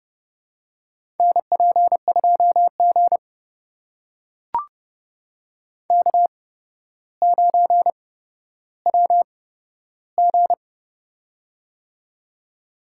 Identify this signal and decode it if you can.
Morse code, heard as a keyed tone, about 30 wpm, spaced as for 13 wpm, 700 Hz. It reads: NP2G K9WG
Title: USA Call Signs - 4 Characters - 8x Character Spacing - 30wpm